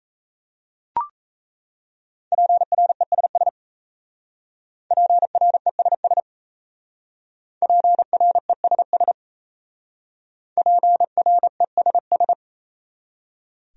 Morse code, heard as a keyed tone, about 35 wpm, 700 Hz. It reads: PRESS PRESS PRESS PRESS
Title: Speed-Racing Top 500 Words 28wpm